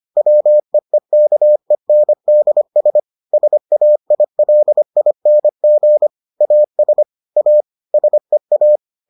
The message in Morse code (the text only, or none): WEEKENDS SAILING AS A SEA